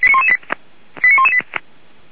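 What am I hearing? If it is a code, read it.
Five-tone selective call (ZVEI), heard as quick runs of five tones, tones of about 70 ms. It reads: 89118
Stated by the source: Five-tone ZVEI call received by SDR, from the fire brigade Feuerwehr Orlamünde server.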